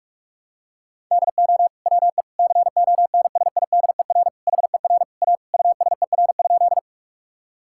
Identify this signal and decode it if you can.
Morse code, heard as a keyed tone, about 45 wpm, 700 Hz. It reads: DO WE CONSIDER HER A USER?
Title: Sentences from Top 1000 Words - Review Part 5 - 45wpm